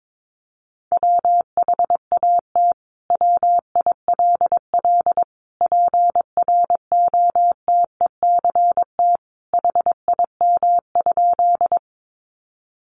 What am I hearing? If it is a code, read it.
Morse code, heard as a keyed tone, about 22 wpm, 700 Hz. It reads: WHAT WILL PROTECT HIM?
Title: Sentences from Top 900 Words 22wpm